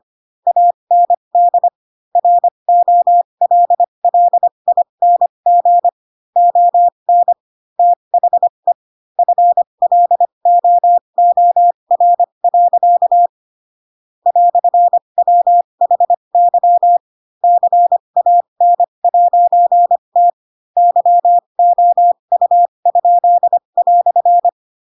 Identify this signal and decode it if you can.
Morse code, heard as a keyed tone, about 25 wpm, spaced as for 22 wpm, 700 Hz. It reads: AND ROLLING ON THE FLOOR. "WHY CAN'T YOU?"
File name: Pinocchio / Pinocchio ch7 18 25 ik1hge